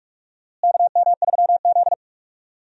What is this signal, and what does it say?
Morse code, heard as a keyed tone, about 46 wpm, 700 Hz. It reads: KM3Z
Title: USA Call Signs - 4 Characters - Repeated 2x - 45wpm